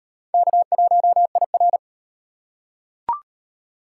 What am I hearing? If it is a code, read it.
Morse code, heard as a keyed tone, about 32 wpm, 700 Hz. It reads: K1IR